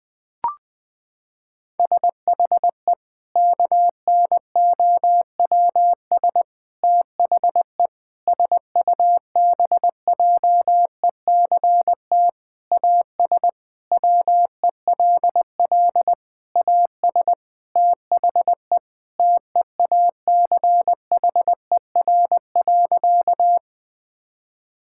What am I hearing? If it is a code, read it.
Morse code, heard as a keyed tone, about 21 wpm, 700 Hz. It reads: SHE KNOWS THE SUBJECT AS WELL AS THE TEACHER.
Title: Sentences with 4-Word Phrases - 20wpm